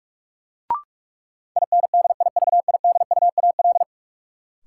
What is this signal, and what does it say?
Morse code, heard as a keyed tone, about 46 wpm, 700 Hz. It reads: INDIVIDUAL
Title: Sentences from Top 1000 Words - Review Part 4 - New Word Drill - 45wpm